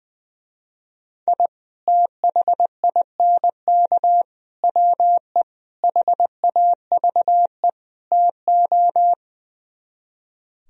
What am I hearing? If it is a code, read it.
Morse code, heard as a keyed tone, about 21 wpm, 700 Hz. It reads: I THINK WE HAVE TO
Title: Speed-Racing - Sets of 5 Words 20wpm